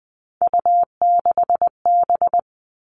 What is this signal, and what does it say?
Morse code, heard as a keyed tone, about 20 wpm, 700 Hz. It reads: U6B
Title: All Cut Numbers - 3-Digit 20wpm